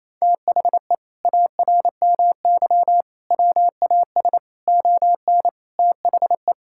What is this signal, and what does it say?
Morse code, heard as a keyed tone, about 28 wpm, 700 Hz. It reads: THE ARMY WAS ON THE
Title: Sentences from Top 1000 Words - Review Part 5 - 28wpm